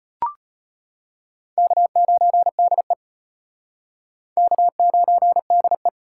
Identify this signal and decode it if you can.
Morse code, heard as a keyed tone, about 34 wpm, 700 Hz. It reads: K9DE K9DE